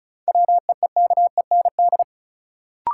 Morse code, heard as a keyed tone, about 36 wpm, 700 Hz. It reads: WEEKEND
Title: Sets of 5 Words - New Word Drill - 35wpm